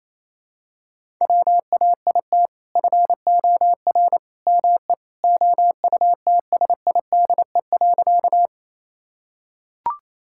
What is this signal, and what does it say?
Morse code, heard as a keyed tone, about 28 wpm, 700 Hz. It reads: WAIT FOR ME OUTSIDE.
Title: Sentences from Top 1000 Words - Review Part 6 - 28wpm